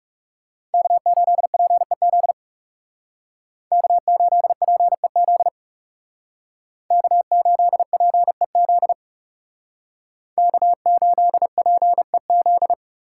Morse code, heard as a keyed tone, about 40 wpm, 700 Hz. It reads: K8PEZ K8PEZ K8PEZ K8PEZ